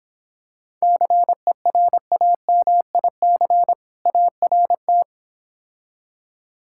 Morse code, heard as a keyed tone, about 25 wpm, 700 Hz. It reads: CERAMIC ART